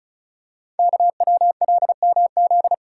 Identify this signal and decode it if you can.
Morse code, heard as a keyed tone, about 36 wpm, 700 Hz. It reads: KWLMZ
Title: Code Groups - 5 Letter:Numbers - 35wpm